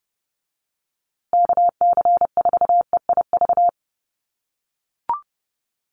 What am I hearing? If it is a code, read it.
Morse code, heard as a keyed tone, about 30 wpm, 700 Hz. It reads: KC4EIV
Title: USA Call Signs Repeated 3x 30wpm